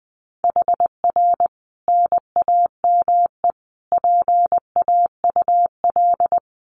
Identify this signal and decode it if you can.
Morse code, heard as a keyed tone, about 19 wpm, 700 Hz. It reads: HR NAME PAUL
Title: Common QSO Phrases 20wpm